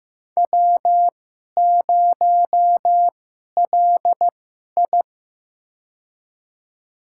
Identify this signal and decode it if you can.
Morse code, heard as a keyed tone, about 15 wpm, spaced as for 11 wpm, 700 Hz. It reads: W0LI
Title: USA Call Signs - 4 Characters - 2x Character Spacing - 15wpm